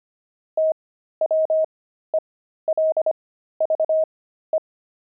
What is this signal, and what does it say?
Morse code, heard as a keyed tone, about 25 wpm, spaced as for 13 wpm, 630 Hz. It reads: TWELVE
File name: pangram-fs-13-cs-25-friars